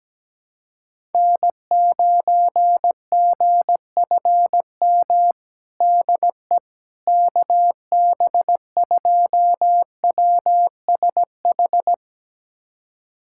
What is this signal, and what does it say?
Morse code, heard as a keyed tone, about 17 wpm, 700 Hz. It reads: N9GFM DE KB2WSH
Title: USA Callsign DE Callsign 17wpm